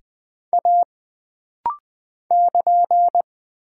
Morse code, heard as a keyed tone, about 20 wpm, 700 Hz. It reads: A <KN>
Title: Single Character (Letter:Number:Punctuation:Prosign) - Rapid-Fire - 20wpm